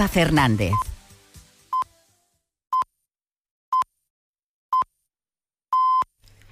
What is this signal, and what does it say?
Radio time signal (time pips), with speech. It is heard from 0.7 to 6.0 s.